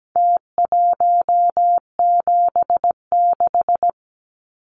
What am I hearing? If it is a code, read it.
Morse code, heard as a keyed tone, about 17 wpm, 700 Hz. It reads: T176
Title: Numbers - Contest Style - 4-Digit 17wpm